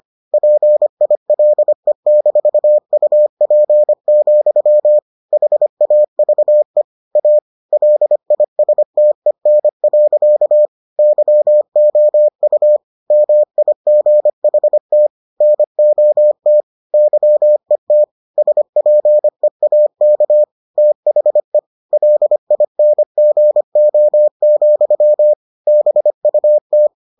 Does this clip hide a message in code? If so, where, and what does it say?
Yes: Morse code from 0.3 to 26.9 s, heard as a keyed tone, about 25 wpm, 600 Hz. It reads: PILE-UP, HAVE A LISTEN. YOU MIGHT NOT YET SPEAK THE LINGO, BUT